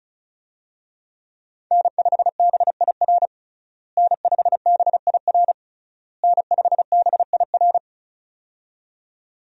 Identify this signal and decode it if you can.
Morse code, heard as a keyed tone, about 36 wpm, 700 Hz. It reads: N5BIR N5BIR N5BIR
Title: USA Call Signs - 5 Characters - Repeated 3x - 35wpm